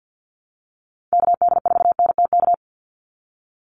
Morse code, heard as a keyed tone, about 50 wpm, 700 Hz. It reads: KD4NTK
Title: USA Call Signs 50wpm